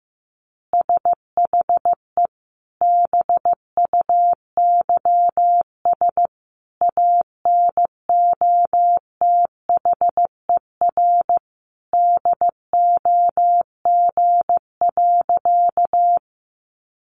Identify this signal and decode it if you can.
Morse code, heard as a keyed tone, about 15 wpm, 700 Hz. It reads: SHE BUYS ANOTHER DOG.
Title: Easy Phrases 15wpm